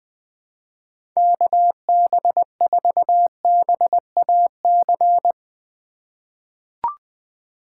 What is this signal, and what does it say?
Morse code, heard as a keyed tone, about 20 wpm, 700 Hz. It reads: KB4BAC
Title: USA Call Signs - 6 Characters - 4x Character Spacing - 20wpm